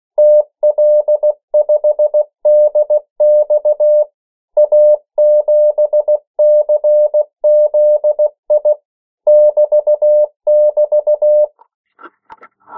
Morse code, heard as a keyed tone, about 16 wpm, 600 Hz. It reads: TL5DX A7CZI <BT><BT>
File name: morse mp3 / 14 urok 13